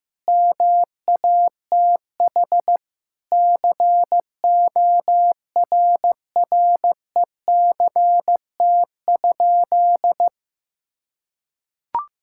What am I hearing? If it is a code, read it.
Morse code, heard as a keyed tone, about 15 wpm, 700 Hz. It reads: MATH CORRECT?
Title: Sentences from Top 1000 Words - Review Part 6 - 15wpm